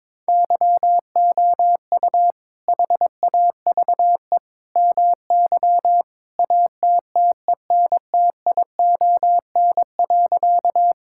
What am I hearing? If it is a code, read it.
Morse code, heard as a keyed tone, about 22 wpm, 700 Hz. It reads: YOU HAVE MY ATTENTION.
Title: Sentences from Top 1000 Words - Review Part 6 - 22wpm